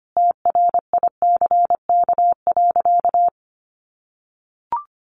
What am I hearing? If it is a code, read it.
Morse code, heard as a keyed tone, about 25 wpm, 700 Hz. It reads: TRICK.
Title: Sentences from Top 1500 Words 25wpm